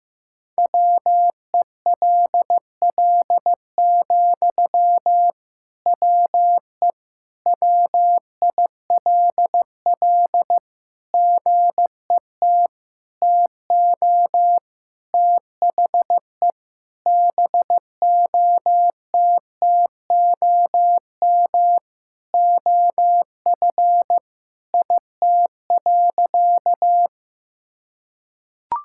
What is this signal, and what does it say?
Morse code, heard as a keyed tone, about 15 wpm, 700 Hz. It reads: WELL, WE WILL GET TO THE BOTTOM OF IT.
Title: Sentences from Top 1000 Words - Review Part 6 - 15wpm